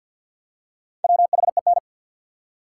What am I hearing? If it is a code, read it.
Morse code, heard as a keyed tone, about 48 wpm, 700 Hz. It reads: WHEN
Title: Top 180 QSO Elements 50wpm